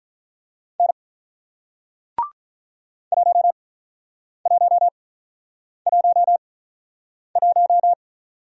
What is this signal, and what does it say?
Morse code, heard as a keyed tone, about 42 wpm, spaced as for 9 wpm, 700 Hz. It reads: N 1111
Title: Speed-Racing Single Letter-Number 35wpm